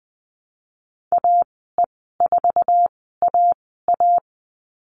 Morse code, heard as a keyed tone, about 19 wpm, spaced as for 14 wpm, 700 Hz. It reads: AE4AA
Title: 069 Lesson 12g - Letter d - Call Signs 20wpm